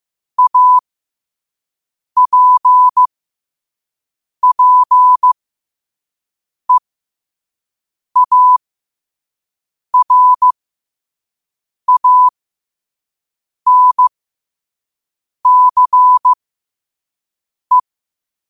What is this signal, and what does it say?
Morse code, heard as a keyed tone, about 15 wpm, spaced as for 5 wpm, 1000 Hz. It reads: APPEARANCE
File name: qotdfiles / 5wpm